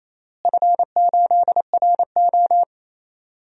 Morse code, heard as a keyed tone, about 28 wpm, 700 Hz. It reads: F8RO
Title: DXCC-100 Call Signs - 4 Characters - Repeated 3x - 28wpm